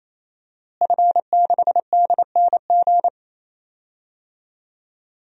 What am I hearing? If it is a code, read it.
Morse code, heard as a keyed tone, about 28 wpm, 700 Hz. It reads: F6DNG
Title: DXCC-100 Call Signs - 5 Characters - Repeated 3x - 28wpm